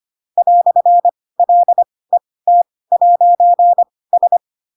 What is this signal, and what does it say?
Morse code, heard as a keyed tone, about 25 wpm, spaced as for 18 wpm, 700 Hz. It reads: "LET'S
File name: Pinocchio / Pinocchio ch2 12 25 ik1hge